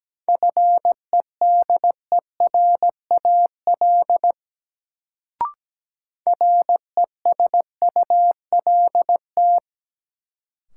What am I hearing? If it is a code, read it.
Morse code, heard as a keyed tone, about 17 wpm, 700 Hz. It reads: FEDERAL RESULT
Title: Sets of 3 Words - New Word Drill - 17wpm